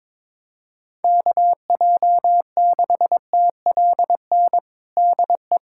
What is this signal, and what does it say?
Morse code, heard as a keyed tone, about 22 wpm, 700 Hz. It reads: KJ6TLN DE
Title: USA Callsign DE Callsign - 6 Characters - 22wpm